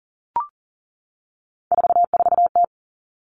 Morse code, heard as a keyed tone, about 38 wpm, 700 Hz. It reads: 44T
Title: Three Challenging Letter-Number Combinations 40wpm